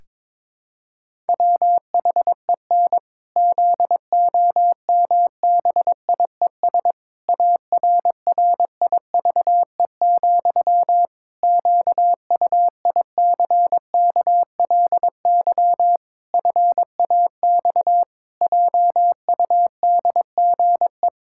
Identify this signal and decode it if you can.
Morse code, heard as a keyed tone, about 21 wpm, 700 Hz. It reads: WHEN ZOMBIES ARRIVE, QUICKLY FAX JUDGE
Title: Pangram Challenge v2 - 22wpm